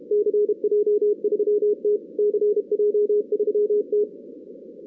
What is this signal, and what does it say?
Morse code, heard as a keyed tone, about 31 wpm, 420 Hz, with radio noise. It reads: CJ3T CJ3T
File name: 2016 arrl dx cw / 2016-ARRL-DX-CW-K5TR- - 0-46